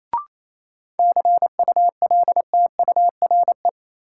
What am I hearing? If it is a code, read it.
Morse code, heard as a keyed tone, about 27 wpm, 700 Hz. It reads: CULTURE
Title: Sentences from Top 1000 Words - Review Part 6 - New Word Drill - 28wpm